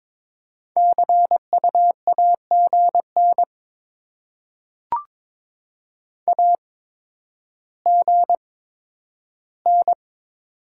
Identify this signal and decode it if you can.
Morse code, heard as a keyed tone, about 22 wpm, spaced as for 11 wpm, 700 Hz. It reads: CU AGN AGN
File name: Top 20 QSO Elements - 8x Character Spacing - 22wpm